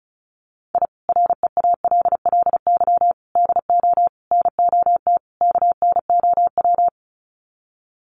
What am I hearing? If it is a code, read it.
Morse code, heard as a keyed tone, about 34 wpm, 700 Hz. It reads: I REALLY DO NOT KNOW
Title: Sets of 5 Words Encore 35wpm